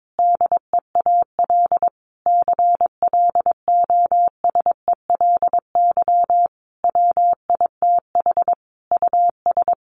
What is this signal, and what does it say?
Morse code, heard as a keyed tone, about 22 wpm, 700 Hz. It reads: DEAL CLOSELY WITH US
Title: Sentences from Top 1400 Words 22wpm